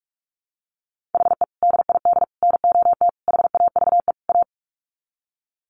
Morse code, heard as a keyed tone, about 44 wpm, 700 Hz. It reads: HE DID NOT HAVE A
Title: Sets of 5 Words Encore 45wpm